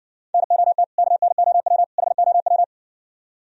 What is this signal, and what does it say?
Morse code, heard as a keyed tone, about 59 wpm, 700 Hz. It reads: NOT KNOW HOW